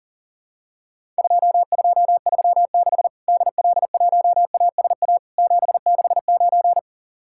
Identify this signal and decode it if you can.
Morse code, heard as a keyed tone, about 44 wpm, 700 Hz. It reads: 2236 DL1ASA 769
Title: Worked All Europe DX Contest - QTC Exchange - 40wpm